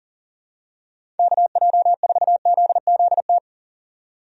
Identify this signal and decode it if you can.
Morse code, heard as a keyed tone, about 40 wpm, 700 Hz. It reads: KJ4ZZT
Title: USA Call Signs - 6 Characters - 40wpm